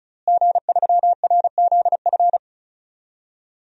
Morse code, heard as a keyed tone, about 34 wpm, 700 Hz. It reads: G3RZF
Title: DXCC-100 Call Signs - 5 Characters - Repeated 2x - 35wpm